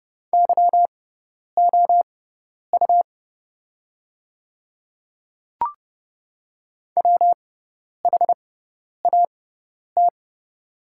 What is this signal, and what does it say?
Morse code, heard as a keyed tone, about 30 wpm, spaced as for 10 wpm, 700 Hz. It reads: YOU WHAT
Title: 6x Character Spacing - Top 100 Words - 30wpm